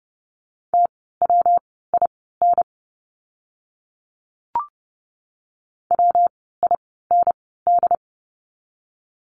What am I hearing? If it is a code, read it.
Morse code, heard as a keyed tone, about 30 wpm, spaced as for 17 wpm, 700 Hz. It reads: TWIN WIND
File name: course / 30:10wpm / 115 Lesson 19f - Letter w - Words 30wpm